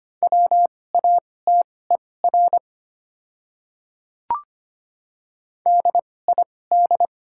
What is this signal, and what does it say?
Morse code, heard as a keyed tone, about 25 wpm, spaced as for 18 wpm, 700 Hz. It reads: WATER DID
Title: 2x Character Spacing - Top 100 Words - 25wpm